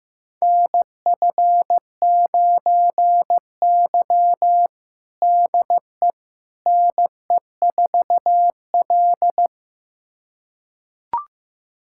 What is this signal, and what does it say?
Morse code, heard as a keyed tone, about 15 wpm, 700 Hz. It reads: NF9Y DE NE4L
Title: USA Callsign DE Callsign - 4 Characters - 15wpm